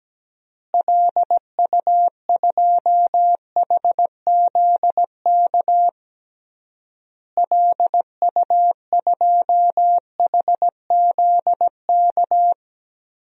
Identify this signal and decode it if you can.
Morse code, heard as a keyed tone, about 17 wpm, 700 Hz. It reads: LU2HZK LU2HZK
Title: DXCC-100 Call Signs - 6 Characters - Repeated 3x - 17wpm